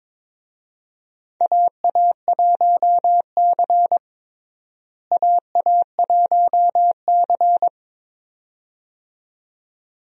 Morse code, heard as a keyed tone, about 22 wpm, 700 Hz. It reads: AA1C AA1C
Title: USA Call Signs - 4 Characters - Repeated 2x - 22wpm